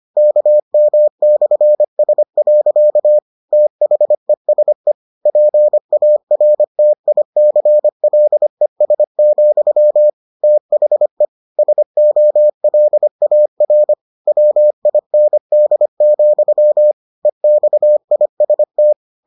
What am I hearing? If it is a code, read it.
Morse code, heard as a keyed tone, about 25 wpm, 600 Hz. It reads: KM/S. THESE PARTICLES, THE SOLAR WIND, EXIST